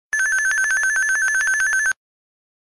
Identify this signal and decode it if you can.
There is no code.